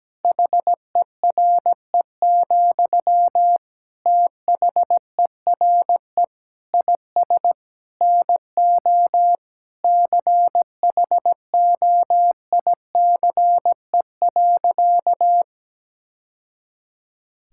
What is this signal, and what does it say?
Morse code, heard as a keyed tone, about 17 wpm, 700 Hz. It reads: HERE, THERE IS NO CHOICE.
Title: Sentences from Top 1000 Words - Review Part 6 - 17wpm